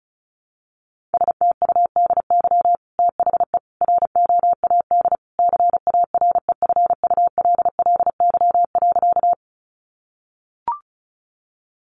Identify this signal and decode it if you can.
Morse code, heard as a keyed tone, about 34 wpm, 700 Hz. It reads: STUDY THE ROAD CAREFULLY.
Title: Sentences from Top 1300 Words 35wpm